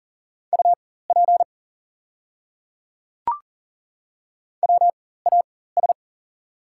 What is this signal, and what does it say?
Morse code, heard as a keyed tone, about 40 wpm, spaced as for 19 wpm, 690 Hz. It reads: UP WAS
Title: 4x Character Spacing - Top 200 Words - 40wpm